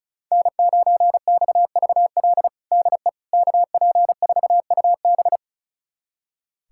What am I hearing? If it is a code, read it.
Morse code, heard as a keyed tone, about 36 wpm, 700 Hz. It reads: N9XVL DE KP4UB
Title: USA Callsign DE Callsign 35wpm